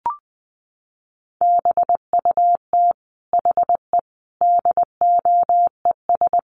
Morse code, heard as a keyed tone, about 20 wpm, 700 Hz. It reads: BUT HE DOES
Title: Sentences from Top 200 Words 20wpm